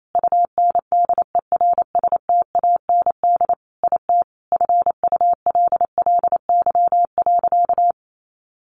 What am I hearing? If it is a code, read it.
Morse code, heard as a keyed tone, about 28 wpm, 700 Hz. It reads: UNDERSTAND IT FULLY.
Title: Sentences from Top 1300 Words 28wpm